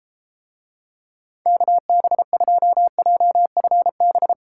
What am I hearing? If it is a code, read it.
Morse code, heard as a keyed tone, about 32 wpm, 700 Hz. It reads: KB2JFB